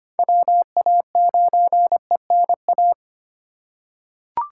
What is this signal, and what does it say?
Morse code, heard as a keyed tone, about 25 wpm, 700 Hz. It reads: WA9ENA
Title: K1USN SST QSO Practice - 25wpm